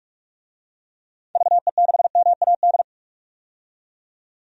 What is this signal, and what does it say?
Morse code, heard as a keyed tone, about 45 wpm, 700 Hz. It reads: VE6MAD
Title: DXCC-100 Call Signs - 6 Characters - 45wpm